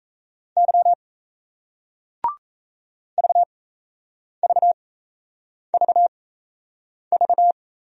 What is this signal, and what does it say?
Morse code, heard as a keyed tone, about 38 wpm, spaced as for 8 wpm, 700 Hz. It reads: Y VVVV